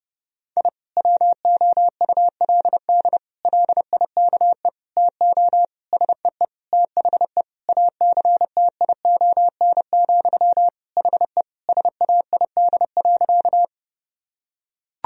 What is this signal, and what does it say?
Morse code, heard as a keyed tone, about 30 wpm, 700 Hz. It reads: I WOULD LIKE TO SEE THE ACTION, HE SAID.
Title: Sentences from Top 1000 Words - Review Part 4 - 30wpm